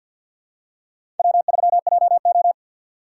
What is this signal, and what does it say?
Morse code, heard as a keyed tone, about 52 wpm, 700 Hz. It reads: W3JO
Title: USA Call Signs - 6x Character Spacing - 50wpm